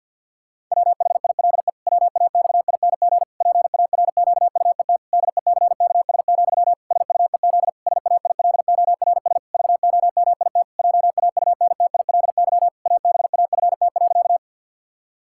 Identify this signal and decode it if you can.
Morse code, heard as a keyed tone, about 51 wpm, 700 Hz. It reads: WHILE WAXING PARQUET DECKS, SUEZ SAILORS VOMIT JAUNTILY ABAFT.
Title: Pangram Challenge v2 - 50wpm